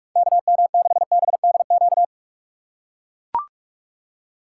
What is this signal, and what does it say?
Morse code, heard as a keyed tone, about 44 wpm, 700 Hz. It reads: KM6BDQ
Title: USA Call Signs - 6 Characters - 6x Character Spacing - 45wpm